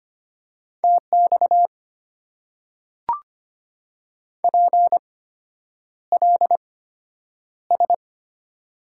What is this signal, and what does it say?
Morse code, heard as a keyed tone, about 25 wpm, spaced as for 9 wpm, 690 Hz. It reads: TX PLS
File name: Top 90 QSO Elements - 8x Character Spacing - 25wpm